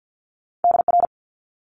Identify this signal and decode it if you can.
Morse code, heard as a keyed tone, about 50 wpm, 700 Hz. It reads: DR
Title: Multi-letter Phonemes Quick Fire 50wpm